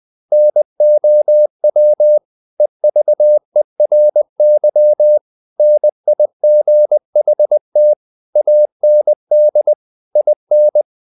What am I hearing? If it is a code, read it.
Morse code, heard as a keyed tone, about 20 wpm, 600 Hz. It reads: NOW EVERY NIGHT AND IN
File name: the old man and the sea / the old man and the sea part 1 20wpm